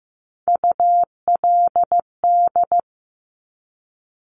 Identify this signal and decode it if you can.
Morse code, heard as a keyed tone, about 15 wpm, 700 Hz. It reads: ULD
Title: Top Three Letter Combinations in Words 15wpm